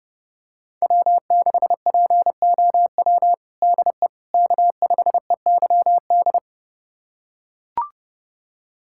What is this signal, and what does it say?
Morse code, heard as a keyed tone, about 32 wpm, 700 Hz. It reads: W6POW DE K5EYD
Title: USA Callsign DE Callsign 30wpm